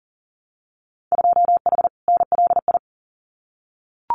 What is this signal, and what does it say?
Morse code, heard as a keyed tone, about 40 wpm, 700 Hz. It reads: 2H NLI
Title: Winter Field Day Exchanges 40wpm